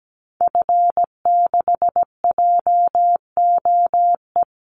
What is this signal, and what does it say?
Morse code, heard as a keyed tone, about 17 wpm, 700 Hz. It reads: F6JOE
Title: Call Signs - 17wpm